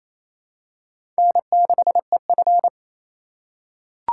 Morse code, heard as a keyed tone, about 28 wpm, 700 Hz. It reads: N6EF
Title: USA Call Signs - 4 Characters - 6x Character Spacing - 28wpm